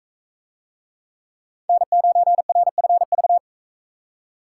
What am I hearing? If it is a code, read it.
Morse code, heard as a keyed tone, about 42 wpm, 700 Hz. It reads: N9RFV